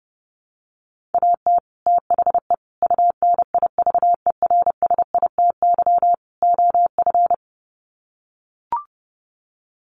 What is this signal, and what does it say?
Morse code, heard as a keyed tone, about 30 wpm, 700 Hz. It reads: AT THE UNIVERSITY OF
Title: Sets of 4 Words 30wpm